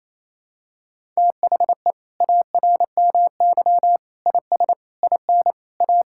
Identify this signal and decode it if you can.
Morse code, heard as a keyed tone, about 27 wpm, 690 Hz. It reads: THE ARMY IS IN A
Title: Sentences from Top 1000 Words - Review Part 6 - 28wpm